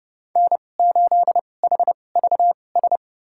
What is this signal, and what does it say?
Morse code, heard as a keyed tone, about 32 wpm, spaced as for 22 wpm, 700 Hz. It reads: N8HVS
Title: USA Call Signs - 5 Characters - 2x Character Spacing - 30wpm